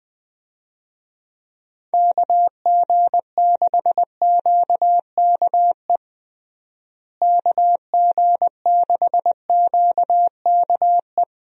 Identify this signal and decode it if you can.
Morse code, heard as a keyed tone, about 20 wpm, 700 Hz. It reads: KG6QKE KG6QKE
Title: USA Call Signs - 6 Characters - Repeated 2x - 20wpm